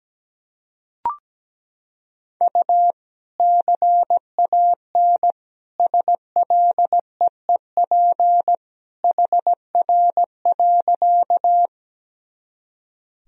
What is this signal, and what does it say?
Morse code, heard as a keyed tone, about 17 wpm, 700 Hz. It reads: U CAN SLEEP HR.
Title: Sentences from Top 500 Words and Abbreviations 17wpm